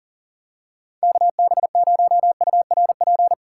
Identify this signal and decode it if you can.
Morse code, heard as a keyed tone, about 40 wpm, 700 Hz. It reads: KB0URP
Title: USA Call Signs - 6 Characters - 6x Character Spacing - 40wpm